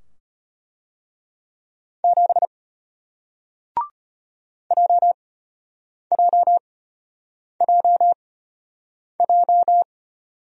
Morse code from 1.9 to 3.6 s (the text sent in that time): 7